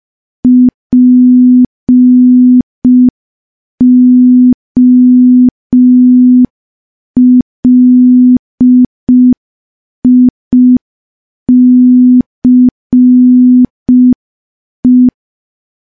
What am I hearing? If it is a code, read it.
Morse code, heard as a keyed tone, about 5 wpm, 250 Hz. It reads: POLICE
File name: morse / morse-training-05WPM